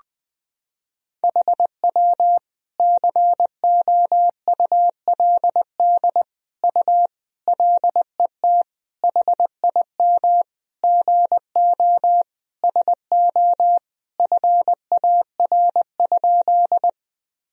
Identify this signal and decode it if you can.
Morse code, heard as a keyed tone, about 21 wpm, 700 Hz. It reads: HW COULD U LET HIM GO SO FAR?
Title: Sentences from Top 300 Words and Abbreviations 20wpm